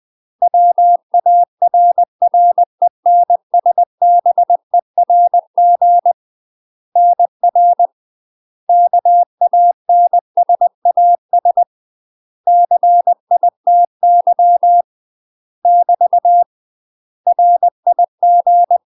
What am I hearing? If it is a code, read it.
Morse code, heard as a keyed tone, about 20 wpm, 700 Hz. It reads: WARRENSBERG NR KANSAS CITY <BT> RIG
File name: UK MASTER QSOs TRAINING / 20WPM / UK MASTER QSOs 0024